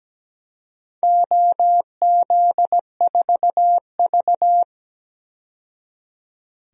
Morse code, heard as a keyed tone, about 17 wpm, 700 Hz. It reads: OZ4V
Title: DXCC-100 Call Signs - 4 Characters - 17wpm